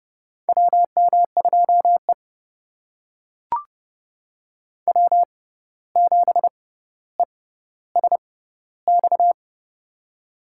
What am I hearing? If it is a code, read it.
Morse code, heard as a keyed tone, about 31 wpm, spaced as for 15 wpm, 700 Hz. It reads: WM2E W7ESX
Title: USA Call Signs - 6x Character Spacing - 30wpm